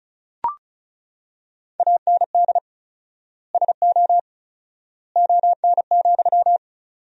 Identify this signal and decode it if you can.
Morse code, heard as a keyed tone, about 36 wpm, 700 Hz. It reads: AND SO ON,
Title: 4x Word-Spacing - Sentences from Top 100 Words - 35wpm